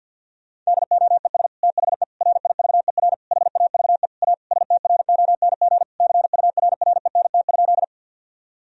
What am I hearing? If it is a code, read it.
Morse code, heard as a keyed tone, about 50 wpm, 700 Hz. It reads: DOES THE RIVER HAVE A STRONG CURRENT?
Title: Sentences from Top 1000 Words - Review Part 4 - 50wpm